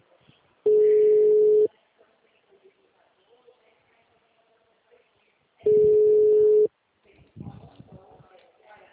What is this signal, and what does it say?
Ringback tone (Europe, 425 Hz), with speech.
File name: out- -2038- - - .18178